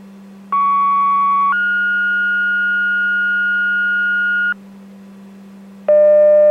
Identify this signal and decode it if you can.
Two-tone page, heard as a one-second tone, then a three-second tone, about 1120 then 1430 Hz, with noise. It is heard from 0.5 to 4.5 s.